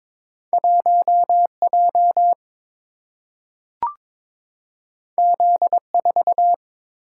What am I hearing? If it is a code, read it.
Morse code, heard as a keyed tone, about 22 wpm, 700 Hz. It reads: 1J Z4
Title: Single-Letter Single-Number 22wpm